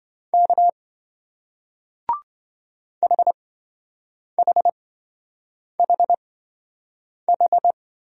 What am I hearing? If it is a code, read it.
Morse code, heard as a keyed tone, about 27 wpm, spaced as for 7 wpm, 700 Hz. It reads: K HHHH